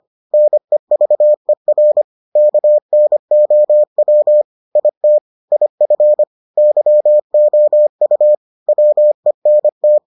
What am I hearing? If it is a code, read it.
Morse code, heard as a keyed tone, about 25 wpm, 600 Hz. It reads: NEVER KNOW IT IF YOU WENT